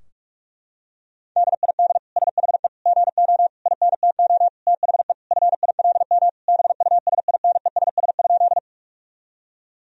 Morse code, heard as a keyed tone, about 46 wpm, 700 Hz. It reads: DID SHE GO INTO THE FILM BUSINESS?
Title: Sentences from Top 1000 Words - Review Part 3 - 45wpm